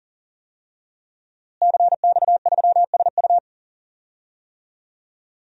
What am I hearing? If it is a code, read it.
Morse code, heard as a keyed tone, about 40 wpm, 700 Hz. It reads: CX3SU
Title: DXCC-100 Call Signs - 5 Characters - Repeated 3x - 40wpm